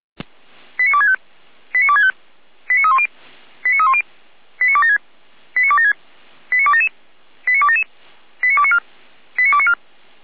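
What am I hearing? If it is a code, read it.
Five-tone selective call (ZVEI), heard as quick runs of five tones, tones of about 70 ms. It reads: 89265 89265 89310 89310 89276 89276 89277 89277 89294 89294
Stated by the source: ZVEI five-tone selective call received by SDR, address 89165.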